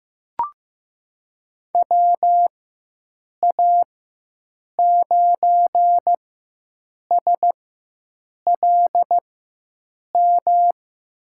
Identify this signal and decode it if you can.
Morse code, heard as a keyed tone, about 15 wpm, spaced as for 7 wpm, 700 Hz. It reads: WA9SLM
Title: USA Call Signs - 6 Characters - 4x Character Spacing - 15wpm